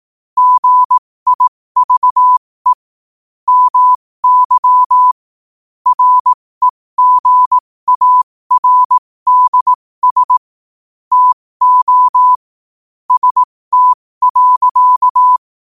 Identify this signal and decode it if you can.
Morse code, heard as a keyed tone, about 18 wpm, spaced as for 15 wpm, 1000 Hz. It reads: GIVE MY REGARDS TO ST.
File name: qotdfiles / 15wpm